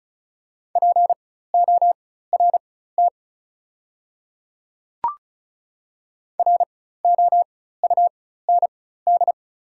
Morse code, heard as a keyed tone, about 34 wpm, spaced as for 16 wpm, 690 Hz. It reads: PORT ROUND
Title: 4x Character Spacing - Top 200 Words - 35wpm